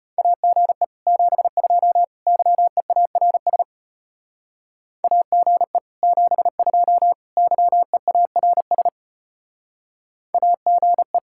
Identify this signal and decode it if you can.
Morse code, heard as a keyed tone, about 31 wpm, 700 Hz. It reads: AGE 72 YEARS AGE 72 YEARS AGE